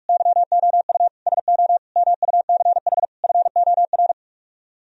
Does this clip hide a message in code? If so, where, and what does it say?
Yes: Morse code from 0.1 to 4.1 s, heard as a keyed tone, about 44 wpm, 700 Hz. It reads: YOU SO MUCH FOR